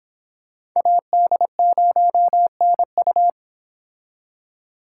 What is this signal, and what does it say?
Morse code, heard as a keyed tone, about 25 wpm, 700 Hz. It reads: AD0NU